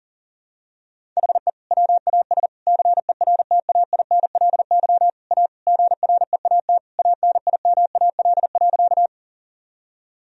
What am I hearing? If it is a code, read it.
Morse code, heard as a keyed tone, about 37 wpm, 700 Hz. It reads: HE WAS CERTAINLY A GREAT ANIMAL.
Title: Sentences from Top 1000 Words - Review Part 1 - 40wpm